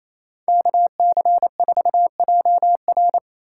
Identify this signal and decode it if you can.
Morse code, heard as a keyed tone, about 38 wpm, spaced as for 33 wpm, 700 Hz. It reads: KC4JR
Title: Speed-Racing Common QSO Phrases Encore - 28wpm